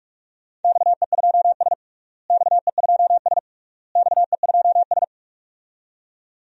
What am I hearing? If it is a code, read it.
Morse code, heard as a keyed tone, about 42 wpm, 700 Hz. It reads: XE2S XE2S XE2S
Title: DXCC-100 Call Signs - 4 Characters - Repeated 3x - 45wpm